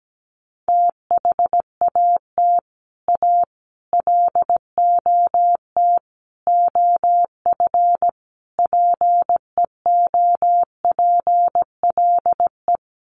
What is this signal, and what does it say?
Morse code, heard as a keyed tone, about 17 wpm, 700 Hz. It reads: THAT A LOT OF PEOPLE
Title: Sets of 5 Words 17wpm